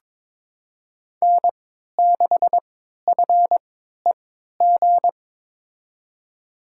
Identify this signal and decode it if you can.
Morse code, heard as a keyed tone, about 21 wpm, spaced as for 12 wpm, 700 Hz. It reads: N6FEG
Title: USA Call Signs - 5 Characters - 3x Character Spacing - 22wpm